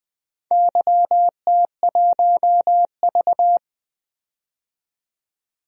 Morse code, heard as a keyed tone, about 20 wpm, 700 Hz. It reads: YT1V
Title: DXCC-100 Call Signs - 4 Characters - 20wpm